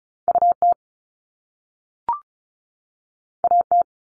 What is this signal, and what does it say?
Morse code, heard as a keyed tone, about 36 wpm, 700 Hz. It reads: U T A T
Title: Top Two Letter Combinations in Words 35wpm